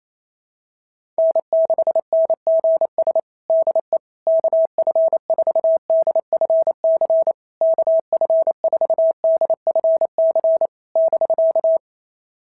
Morse code, heard as a keyed tone, about 28 wpm, 650 Hz. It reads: N6NGS DE KF4DFC KF4DFC <BK>
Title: SOTA QSO Practice 28wpm